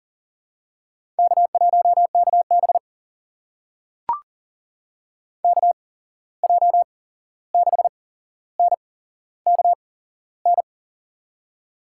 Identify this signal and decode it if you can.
Morse code, heard as a keyed tone, about 40 wpm, spaced as for 15 wpm, 700 Hz. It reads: K1KB KJ6NKN
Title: USA Call Signs - 8x Character Spacing - 40wpm